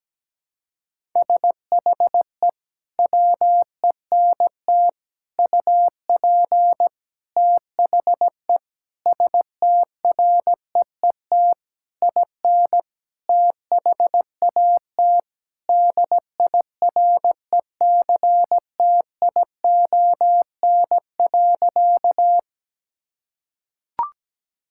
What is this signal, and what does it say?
Morse code, heard as a keyed tone, about 17 wpm, 700 Hz. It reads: SHE WENT UP THE STREET IN THAT DIRECTION.
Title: Sentences from Top 1000 Words - Review Part 5 - 17wpm